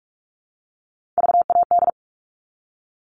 Morse code, heard as a keyed tone, about 44 wpm, 700 Hz. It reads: VAD
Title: All Cut Numbers - 3-Digit 45wpm